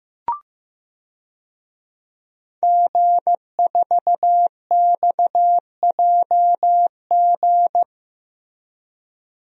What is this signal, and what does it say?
Morse code, heard as a keyed tone, about 15 wpm, 700 Hz. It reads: G4XJG
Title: DXCC-100 Call Signs - 5 Characters - Repeated 3x - 15wpm